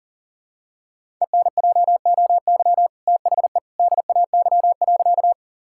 Morse code, heard as a keyed tone, about 40 wpm, 700 Hz. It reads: ENJOY THE DAY.
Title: Sentences from Top 1000 Words - Review Part 5 - 40wpm